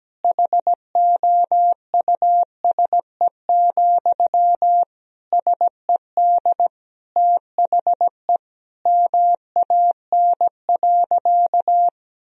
Morse code, heard as a keyed tone, about 17 wpm, 700 Hz. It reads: HOUSE, SED THE MAN.
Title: Sentences from Top 400 Words and Abbreviations 17wpm